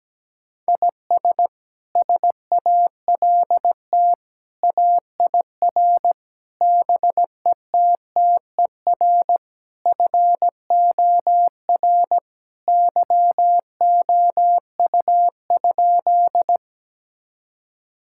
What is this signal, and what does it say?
Morse code, heard as a keyed tone, about 17 wpm, 700 Hz. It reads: IS SALT AIR BETTER FOR YOU?
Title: Sentences from Top 1000 Words - Review Part 5 - 17wpm